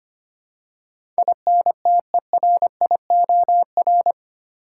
Morse code, heard as a keyed tone, about 25 wpm, 700 Hz. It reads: INTERIOR
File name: 8 Letter Words - First 3 Letters Given - 25wpm